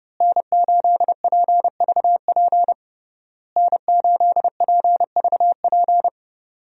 Morse code, heard as a keyed tone, about 30 wpm, 700 Hz. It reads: N8PVP N8PVP
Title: USA Call Signs - 5 Characters - Repeated 3x - 30wpm